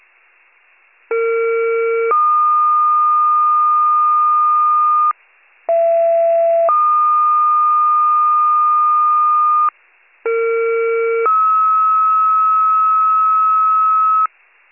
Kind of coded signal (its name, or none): two-tone page